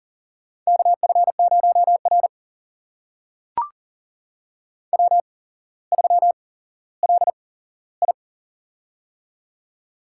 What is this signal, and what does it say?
Morse code, heard as a keyed tone, about 43 wpm, spaced as for 17 wpm, 700 Hz. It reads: KF0R W3LI
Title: USA Call Signs - 4 Characters - 8x Character Spacing - 40wpm